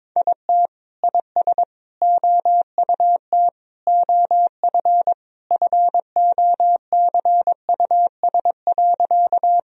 Morse code, heard as a keyed tone, about 21 wpm, 700 Hz. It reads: IT IS OUT OF FOCUS.
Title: Sentences from Top 1000 Words - Review Part 4 - 22wpm